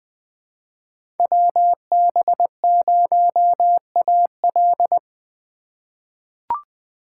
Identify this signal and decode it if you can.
Morse code, heard as a keyed tone, about 20 wpm, 700 Hz. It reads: WB0AL
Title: USA Call Signs - 5 Characters - 4x Character Spacing - 20wpm